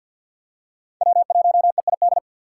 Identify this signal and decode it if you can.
Morse code, heard as a keyed tone, about 50 wpm, 700 Hz. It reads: W1EID
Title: USA Call Signs - 5 Characters - 3x Character Spacing - 50wpm